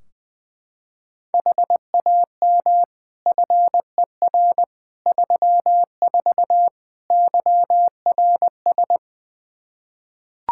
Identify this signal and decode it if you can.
Morse code, heard as a keyed tone, about 21 wpm, 700 Hz. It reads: HAM FER 34 YRS
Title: Practice QSO - Exchange 3 - More Info - 20wpm